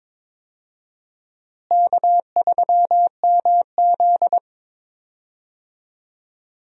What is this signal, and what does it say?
Morse code, heard as a keyed tone, about 21 wpm, 700 Hz. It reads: K3MZ
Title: USA Call Signs - 4 Characters - Repeated 2x - 22wpm